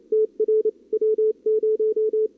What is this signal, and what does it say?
Morse code, heard as a keyed tone, about 32 wpm, 440 Hz. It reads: TR W0
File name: CWSprint / Feb2022 / k5tr-feb-na-sprint-cw-k5ot -0000